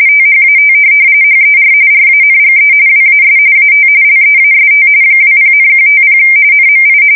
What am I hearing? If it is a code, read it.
RTTY, about 45.45 baud. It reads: THAT HER RANGE WAS AT LEAST 20,150 NMI
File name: fsk170 / USS-Missouri